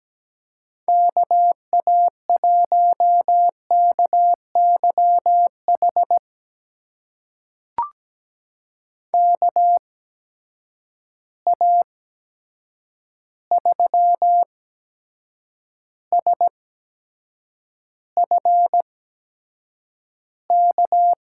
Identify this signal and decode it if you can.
Morse code, heard as a keyed tone, about 17 wpm, spaced as for 7 wpm, 700 Hz. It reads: KA1KYH KA3SFK
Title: USA Call Signs - 8x Character Spacing - 17wpm